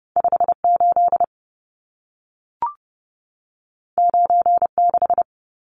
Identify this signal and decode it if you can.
Morse code, heard as a keyed tone, about 30 wpm, 700 Hz. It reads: 58 96
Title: Numbers - 2-Digit 30wpm